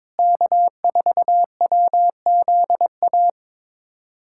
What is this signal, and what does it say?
Morse code, heard as a keyed tone, about 22 wpm, 700 Hz. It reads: K4WZA
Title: USA Callsign DE Callsign - 5 Characters - 22wpm